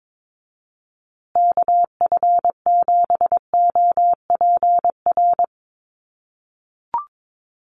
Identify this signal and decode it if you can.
Morse code, heard as a keyed tone, about 22 wpm, 700 Hz. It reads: KF7OPR
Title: USA Call Signs Repeated 2x 22wpm